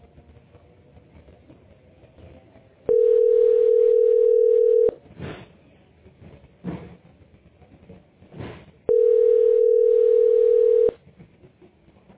Ringback tone (North America), 440 + 480 Hz.